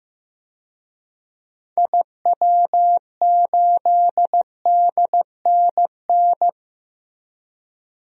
Morse code, heard as a keyed tone, about 15 wpm, 700 Hz. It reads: IW8DNN
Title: DXCC-100 Call Signs - 6 Characters - Repeated 3x - 15wpm